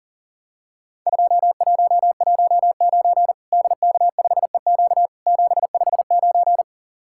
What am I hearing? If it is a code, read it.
Morse code, heard as a keyed tone, about 37 wpm, 700 Hz. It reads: 2119 DK5EQ 759
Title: Worked All Europe DX Contest - QTC Exchange - 40wpm